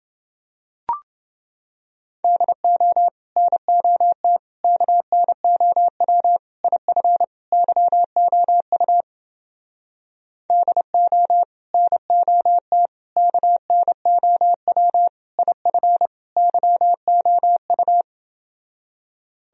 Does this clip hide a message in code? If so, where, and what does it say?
Yes: Morse code from 2.2 to 18.0 s, heard as a keyed tone, about 29 wpm, 700 Hz. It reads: DO NOT KNOW IF YOU DO NOT KNOW IF YOU